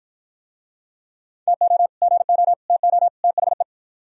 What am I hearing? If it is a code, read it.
Morse code, heard as a keyed tone, about 52 wpm, 700 Hz. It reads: TO GO TO THE